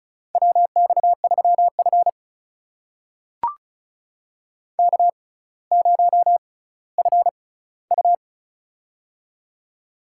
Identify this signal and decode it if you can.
Morse code, heard as a keyed tone, about 36 wpm, spaced as for 18 wpm, 700 Hz. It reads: WX3F K0FU
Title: USA Call Signs - 4 Characters - 6x Character Spacing - 35wpm